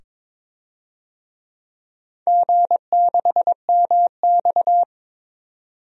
Morse code, heard as a keyed tone, about 22 wpm, 700 Hz. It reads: G6MX
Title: DXCC-100 Call Signs - 4 Characters - 22wpm